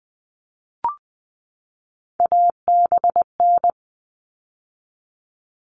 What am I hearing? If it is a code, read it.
Morse code, heard as a keyed tone, about 20 wpm, 700 Hz. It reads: ABN
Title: All Cut Numbers - 3-Digit 20wpm